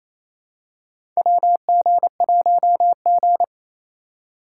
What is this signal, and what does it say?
Morse code, heard as a keyed tone, about 28 wpm, 700 Hz. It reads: WG1G
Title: USA Call Signs - 4 Characters - 2x Character Spacing - 28wpm